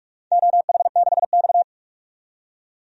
Morse code, heard as a keyed tone, about 46 wpm, 700 Hz. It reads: OH6X
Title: DXCC-100 Call Signs - 4 Characters - 45wpm